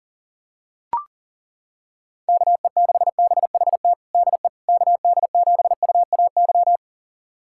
Morse code, heard as a keyed tone, about 40 wpm, 700 Hz. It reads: KE6BHT DE KD7UAY
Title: USA Callsign DE Callsign 40wpm